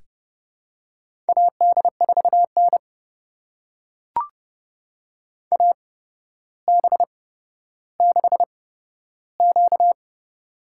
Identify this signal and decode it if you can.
Morse code, heard as a keyed tone, about 30 wpm, spaced as for 13 wpm, 700 Hz. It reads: AD4N AB6Q
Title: USA Call Signs - 4 Characters - 8x Character Spacing - 30wpm